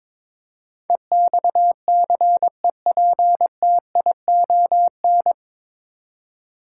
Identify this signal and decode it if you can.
Morse code, heard as a keyed tone, about 22 wpm, 700 Hz. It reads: EXCEPTION
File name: practice / 22wpm / Nine Letter Words - 22wpm